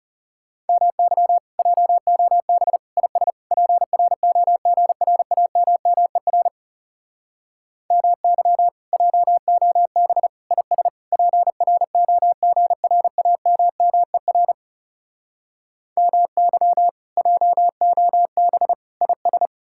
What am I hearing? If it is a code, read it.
Morse code, heard as a keyed tone, about 37 wpm, 700 Hz. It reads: MY JOB IS PROGRAMMER MY JOB IS PROGRAMMER MY JOB IS
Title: Speed-Racing Common QSO Phrases Encore - 30wpm